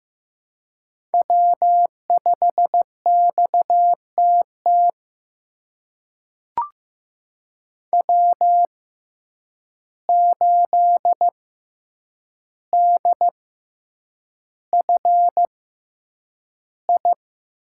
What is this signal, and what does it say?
Morse code, heard as a keyed tone, about 15 wpm, spaced as for 8 wpm, 700 Hz. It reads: W5XTT W8DFI
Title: USA Call Signs - 5 Characters - 6x Character Spacing - 15wpm